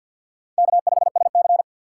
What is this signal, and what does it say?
Morse code, heard as a keyed tone, about 47 wpm, 700 Hz. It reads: K5SC
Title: USA Call Signs - 3x Character Spacing - 50wpm